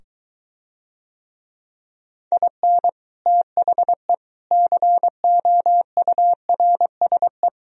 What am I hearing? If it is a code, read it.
Morse code, heard as a keyed tone, about 23 wpm, 700 Hz. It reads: IN THE COURSE